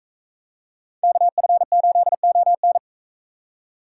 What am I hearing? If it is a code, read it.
Morse code, heard as a keyed tone, about 42 wpm, 700 Hz. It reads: KF8ON